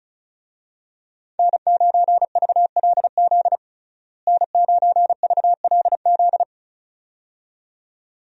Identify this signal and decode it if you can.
Morse code, heard as a keyed tone, about 36 wpm, 700 Hz. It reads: N9VLZ N9VLZ
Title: USA Call Signs - 5 Characters - Repeated 2x - 35wpm